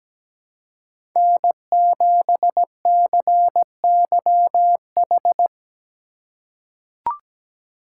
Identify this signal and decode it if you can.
Morse code, heard as a keyed tone, about 17 wpm, 700 Hz. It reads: N7CYH
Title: USA Call Signs - 5 Characters - 4x Character Spacing - 17wpm